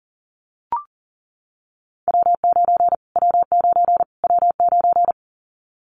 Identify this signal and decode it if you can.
Morse code, heard as a keyed tone, about 39 wpm, 700 Hz. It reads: W9 W9 W9
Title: Callsign Prefixes for Top 25 DXCC - Repeated 3x - 40wpm